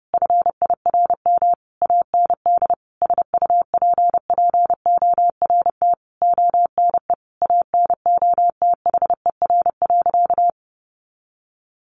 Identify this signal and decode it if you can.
Morse code, heard as a keyed tone, about 30 wpm, 700 Hz. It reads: FIRM AND SUPPORT ONE ANOTHER.
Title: Sentences from Top 1400 Words 30wpm